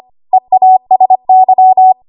Morse code, heard as a keyed tone, about 25 wpm, 750 Hz. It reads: EASY
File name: Archive / 25 WPM / 25WPM